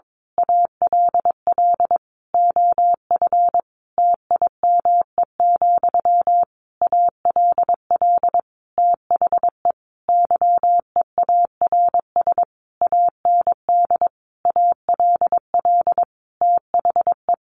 Morse code, heard as a keyed tone, about 22 wpm, 700 Hz. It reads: ALL OF TIME, ALL THE YEARS AND ALL THE
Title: A Sound Of Thunder 22wpm